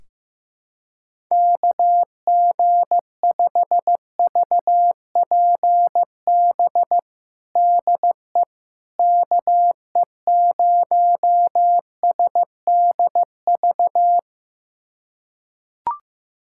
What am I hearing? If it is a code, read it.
Morse code, heard as a keyed tone, about 15 wpm, 700 Hz. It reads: KG5VPB DE KE0SDV
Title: USA Callsign DE Callsign 15wpm